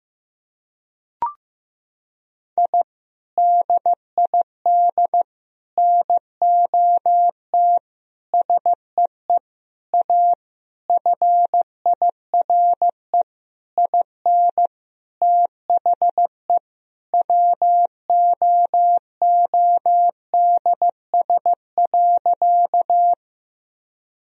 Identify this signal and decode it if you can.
Morse code, heard as a keyed tone, about 15 wpm, 700 Hz. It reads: I DID NOT SEE A FIRE IN THE WOODS.
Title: Sentences from Top 1000 Words - Review Part 2 - 15wpm